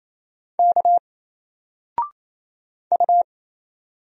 Morse code, heard as a keyed tone, about 27 wpm, 700 Hz. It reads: K U
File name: Single-Letter-Number-28wpm